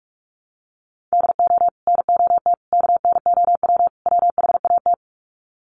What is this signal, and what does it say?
Morse code, heard as a keyed tone, about 44 wpm, 700 Hz. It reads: DO NOT KNOW WHAT
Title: Sets of 4 Words 45wpm